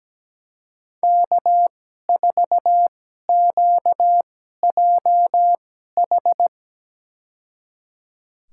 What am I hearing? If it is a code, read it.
Morse code, heard as a keyed tone, about 17 wpm, spaced as for 12 wpm, 700 Hz. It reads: K4QJH
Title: USA Call Signs - 2x Character Spacing - 17wpm